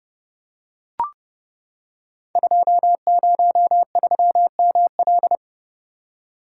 Morse code, heard as a keyed tone, about 28 wpm, 700 Hz. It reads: 203ML
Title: Code Groups - 5 Letter:Numbers - 30wpm